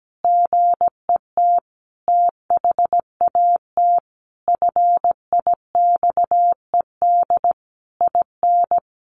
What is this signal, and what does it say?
Morse code, heard as a keyed tone, about 17 wpm, 700 Hz. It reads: GET THAT FIXED IN
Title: Sentences from Top 1100 Words 17wpm